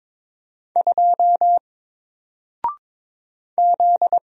Morse code, heard as a keyed tone, about 22 wpm, 700 Hz. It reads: 2 Z
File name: Single-Letter-Number-22wpm